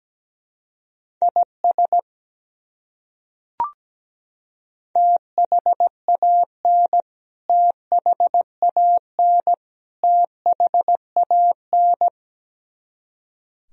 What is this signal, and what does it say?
Morse code, heard as a keyed tone, about 17 wpm, 700 Hz. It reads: IS THAN THAN THAN
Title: Top 100 Words Repeated 17wpm